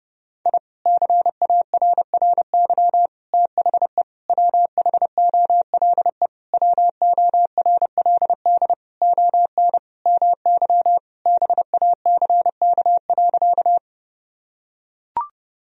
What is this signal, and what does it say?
Morse code, heard as a keyed tone, about 30 wpm, 700 Hz. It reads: I CARRY THE WHOLE WORLD ON MY BACK.
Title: Sentences from Top 1000 Words - Review Part 1 - 30wpm